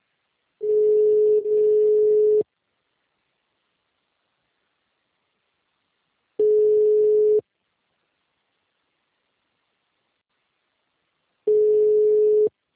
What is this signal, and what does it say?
Ringback tone (Europe, 425 Hz).